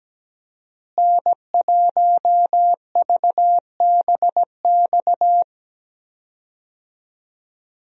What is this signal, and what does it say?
Morse code, heard as a keyed tone, about 17 wpm, 700 Hz. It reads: N1VBX
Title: USA Call Signs - 5 Characters - Repeated 2x - 17wpm